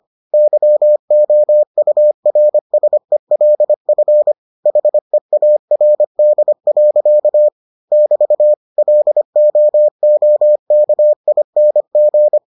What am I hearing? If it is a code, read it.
Morse code, heard as a keyed tone, about 25 wpm, 600 Hz. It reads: YOURSELF HEARD. <BT> LOOKING